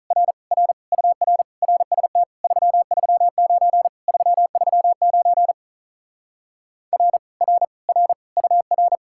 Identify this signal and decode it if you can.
Morse code, heard as a keyed tone, about 44 wpm, 700 Hz. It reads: R R UR RST 339 339 R R R UR